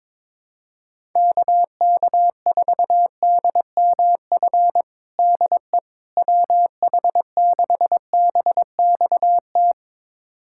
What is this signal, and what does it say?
Morse code, heard as a keyed tone, about 22 wpm, 700 Hz. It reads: KK4DMF DE WH6BXT
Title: USA Callsign DE Callsign 22wpm